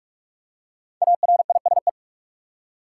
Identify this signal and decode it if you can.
Morse code, heard as a keyed tone, about 46 wpm, 700 Hz. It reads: ARISE
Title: Five Letter Words - 45wpm